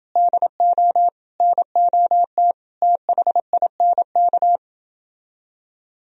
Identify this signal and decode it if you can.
Morse code, heard as a keyed tone, about 28 wpm, 700 Hz. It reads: DO NOT THINK